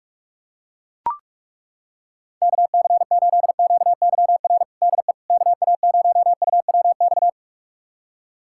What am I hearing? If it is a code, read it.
Morse code, heard as a keyed tone, about 41 wpm, 690 Hz. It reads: KC8QYR DE KA0UWX
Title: USA Callsign DE Callsign - 6 Characters - 45wpm